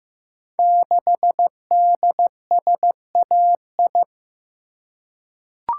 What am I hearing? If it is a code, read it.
Morse code, heard as a keyed tone, about 15 wpm, 700 Hz. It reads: 6DSAI
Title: Code Groups - 5 Letter:Numbers - 15wpm